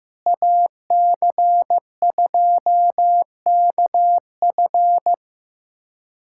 Morse code, heard as a keyed tone, about 15 wpm, 700 Hz. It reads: AC2KF
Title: USA Callsign DE Callsign - 5 Characters - 15wpm